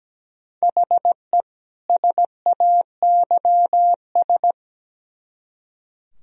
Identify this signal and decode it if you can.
Morse code, heard as a keyed tone, about 20 wpm, 700 Hz. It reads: HE SAYS
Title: Speed-Racing - Sets of 2 Words 17wpm